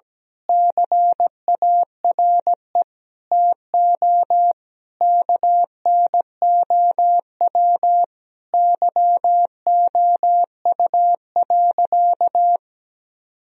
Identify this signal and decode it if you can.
Morse code, heard as a keyed tone, about 17 wpm, 700 Hz. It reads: CARE TO KNOW YOU.
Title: Sentences from Top 1000 Words - Review Part 1 - 17wpm